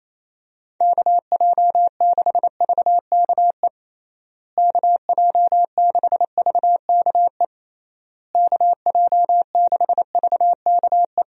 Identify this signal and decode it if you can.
Morse code, heard as a keyed tone, about 28 wpm, 700 Hz. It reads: KJ6VKE KJ6VKE KJ6VKE
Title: USA Call Signs - 6 Characters - Repeated 3x - 28wpm